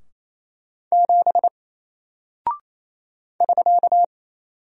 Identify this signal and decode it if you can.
Morse code, heard as a keyed tone, about 28 wpm, 700 Hz. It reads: 7 <SK>
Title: Single Character (Letter:Number:Punctuation:Prosign) Alt 28wpm